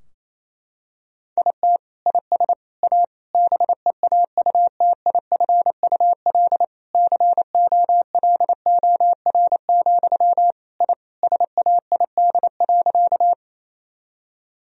Morse code, heard as a keyed tone, about 28 wpm, 700 Hz. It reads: IT IS A BEAUTIFUL COLOR, I SAID.
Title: Sentences from Top 1000 Words - Review Part 2 - 28wpm